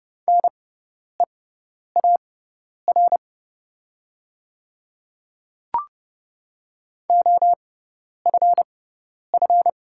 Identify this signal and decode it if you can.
Morse code, heard as a keyed tone, about 30 wpm, spaced as for 10 wpm, 690 Hz. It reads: NEAR OFF
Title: 6x Character Spacing - Top 500 Words - 30wpm